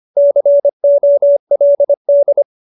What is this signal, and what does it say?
Morse code, heard as a keyed tone, about 25 wpm, 570 Hz. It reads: COLD
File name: QSO303 25